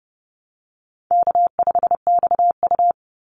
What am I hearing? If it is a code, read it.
Morse code, heard as a keyed tone, about 30 wpm, 700 Hz. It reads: K5XU
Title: Call Signs Repeated 3x - 30wpm